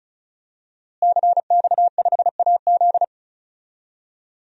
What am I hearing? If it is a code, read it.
Morse code, heard as a keyed tone, about 36 wpm, 700 Hz. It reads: CX5AZ
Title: DXCC-100 Call Signs - 5 Characters - Repeated 3x - 35wpm